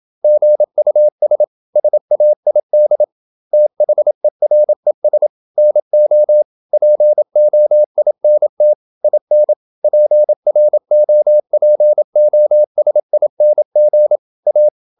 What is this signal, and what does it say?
Morse code, heard as a keyed tone, about 27 wpm, 600 Hz. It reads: GUS SAID THERES NO POINT IN PROPOSING A
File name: Never 0027wpm